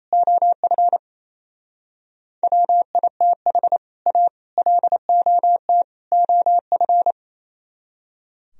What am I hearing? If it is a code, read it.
Morse code, heard as a keyed tone, about 33 wpm, 700 Hz. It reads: OF WITH A LOT OF
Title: Speed-Racing - Sets of 4 Words 28wpm